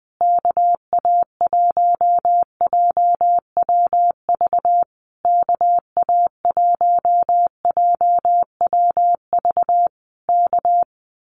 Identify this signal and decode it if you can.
Morse code, heard as a keyed tone, about 20 wpm, 700 Hz. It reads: KA1JWV KA1JWV K
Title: Common QSO Phrases Encore 20wpm